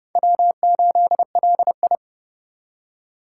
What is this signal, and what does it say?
Morse code, heard as a keyed tone, about 31 wpm, 700 Hz. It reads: W8LI
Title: USA Call Signs - 4x Character Spacing - 30wpm